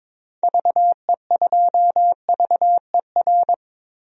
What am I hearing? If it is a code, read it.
Morse code, heard as a keyed tone, about 22 wpm, 700 Hz. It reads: VE2VER
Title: DXCC-100 Call Signs - 6 Characters - 22wpm